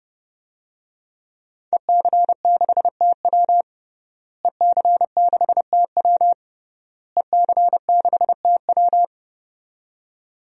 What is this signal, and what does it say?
Morse code, heard as a keyed tone, about 30 wpm, 700 Hz. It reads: EC6TW EC6TW EC6TW
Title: DXCC-100 Call Signs - 5 Characters - Repeated 3x - 30wpm